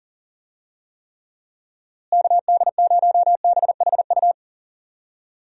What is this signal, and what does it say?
Morse code, heard as a keyed tone, about 40 wpm, 700 Hz. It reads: KD0BHU
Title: USA Call Signs - 6 Characters - Repeated 2x - 40wpm